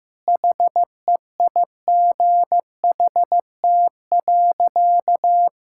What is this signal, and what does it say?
Morse code, heard as a keyed tone, about 15 wpm, 700 Hz. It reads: HEIGHT.
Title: Sentences from Top 1600 Words 15wpm